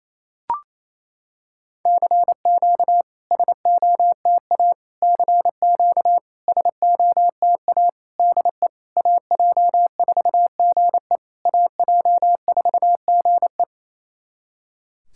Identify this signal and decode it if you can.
Morse code, heard as a keyed tone, about 28 wpm, 690 Hz. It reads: CQ SOTA CQ SOTA DE AJ4GE AJ4GE
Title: SOTA QSO Practice 28wpm